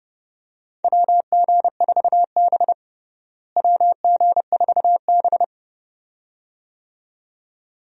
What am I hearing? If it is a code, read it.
Morse code, heard as a keyed tone, about 30 wpm, 700 Hz. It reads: WG4B WG4B
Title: USA Call Signs - 4 Characters - Repeated 2x - 30wpm